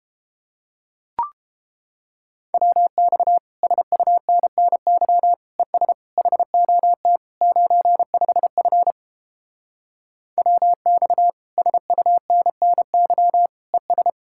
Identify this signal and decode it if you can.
Morse code, heard as a keyed tone, about 29 wpm, 700 Hz. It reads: WX SUNNY ES HOT 95F WX SUNNY ES